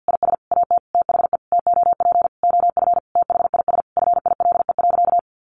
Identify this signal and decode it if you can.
Morse code, heard as a keyed tone, about 50 wpm, 700 Hz. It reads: IS AT THE TOP OF THIS FILE.
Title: Sentences from Top 1200 Words 50wpm